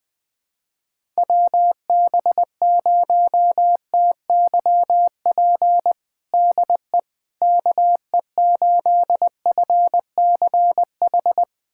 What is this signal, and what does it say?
Morse code, heard as a keyed tone, about 20 wpm, 700 Hz. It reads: WB0TYP DE KE8FCH
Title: USA Callsign DE Callsign - 6 Characters - 20wpm